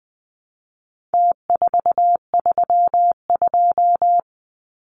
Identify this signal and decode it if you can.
Morse code, heard as a keyed tone, about 20 wpm, 700 Hz. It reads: T432
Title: Numbers - Contest Style - 4-Digit 20wpm